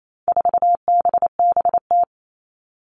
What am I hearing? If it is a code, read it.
Morse code, heard as a keyed tone, about 28 wpm, 700 Hz. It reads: 4BBT
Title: All Cut Numbers - 4-Digit 28wpm